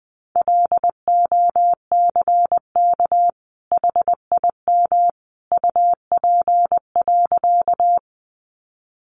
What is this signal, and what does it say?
Morse code, heard as a keyed tone, about 20 wpm, 700 Hz. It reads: LOCK HIM UP.
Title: Sentences from Top 1300 Words 20wpm